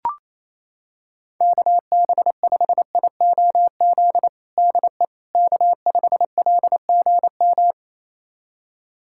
Morse code, heard as a keyed tone, about 28 wpm, 700 Hz. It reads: KB5IOZ DE K5LGM
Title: USA Callsign DE Callsign 28wpm